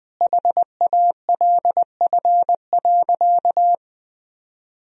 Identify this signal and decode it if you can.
Morse code, heard as a keyed tone, about 20 wpm, 700 Hz. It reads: HALF.
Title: Sentences from Top 1000 Words - Review Part 2 - 20wpm